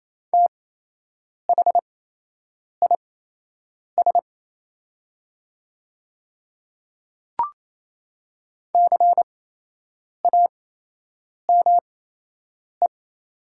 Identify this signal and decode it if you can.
Morse code, heard as a keyed tone, about 28 wpm, spaced as for 8 wpm, 700 Hz. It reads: THIS CAME
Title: 8x Character Spacing - Top 500 Words - 28wpm